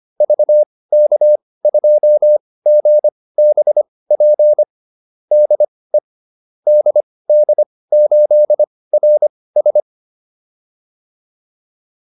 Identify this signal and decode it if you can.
Morse code, heard as a keyed tone, about 25 wpm, spaced as for 18 wpm, 600 Hz. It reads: VK2GBP DE DD8RS
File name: QSO405 18